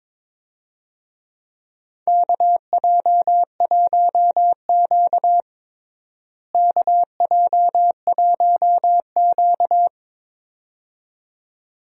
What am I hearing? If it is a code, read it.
Morse code, heard as a keyed tone, about 22 wpm, 700 Hz. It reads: KJ1Q KJ1Q
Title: USA Call Signs - 4 Characters - Repeated 2x - 22wpm